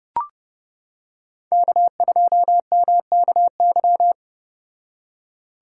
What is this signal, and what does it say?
Morse code, heard as a keyed tone, about 31 wpm, 700 Hz. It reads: K2MKY